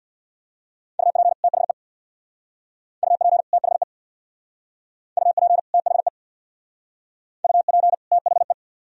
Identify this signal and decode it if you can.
Morse code, heard as a keyed tone, about 58 wpm, 700 Hz. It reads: UP THE UP THE UP THE UP THE
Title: Speed-Racing - Sets of 2 Words 50wpm